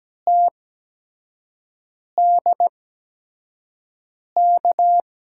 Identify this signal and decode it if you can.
Morse code, heard as a keyed tone, about 17 wpm, spaced as for 5 wpm, 700 Hz. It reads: TDK
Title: USA Call Signs - 5 Characters - 8x Character Spacing - 17wpm